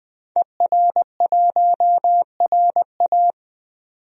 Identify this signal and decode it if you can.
Morse code, heard as a keyed tone, about 20 wpm, 700 Hz. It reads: ER1RA
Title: DXCC-100 Call Signs - 5 Characters - 20wpm